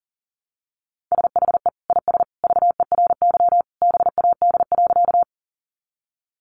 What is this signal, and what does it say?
Morse code, heard as a keyed tone, about 40 wpm, 700 Hz. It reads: SHE IS VERY BAD.
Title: Sentences from Top 700 Words 40wpm